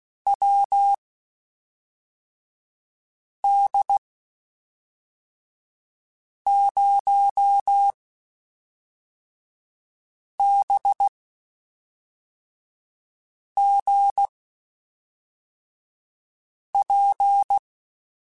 Morse code, heard as a keyed tone, about 16 wpm, spaced as for 3 wpm, 800 Hz. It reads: WD0BGP
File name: QSOs / QSOs-mp3s / WD0BGP 1@5